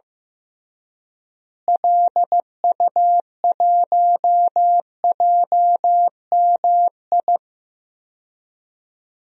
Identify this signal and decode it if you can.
Morse code, heard as a keyed tone, about 15 wpm, 700 Hz. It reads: LU1JMI
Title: DXCC-100 Call Signs - 6 Characters - Repeated 2x - 15wpm